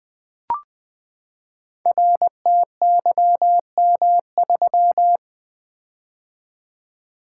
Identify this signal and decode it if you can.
Morse code, heard as a keyed tone, about 20 wpm, 700 Hz. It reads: RTYM3
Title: Code Groups - 5 Letter:Numbers - 20wpm